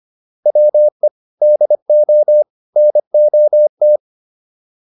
Morse code, heard as a keyed tone, about 25 wpm, 600 Hz. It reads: WE DO NOT